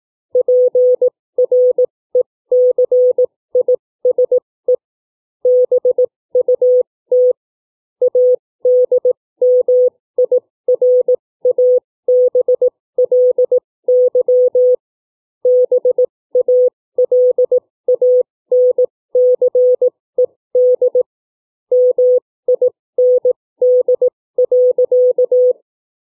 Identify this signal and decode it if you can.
Morse code, heard as a keyed tone, about 18 wpm, spaced as for 15 wpm, 500 Hz. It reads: PRECISE BUT ADMIRABLY BALANCED MIND.
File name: CW / book reading